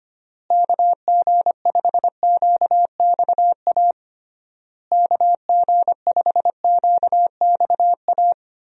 Morse code, heard as a keyed tone, about 25 wpm, 700 Hz. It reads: KG5QXA KG5QXA
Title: USA Call Signs - 6 Characters - Repeated 3x - 25wpm